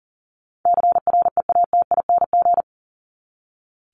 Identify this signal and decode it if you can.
Morse code, heard as a keyed tone, about 40 wpm, 700 Hz. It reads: CREATING
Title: ING Suffix - Top 100 Words with Base Words 40wpm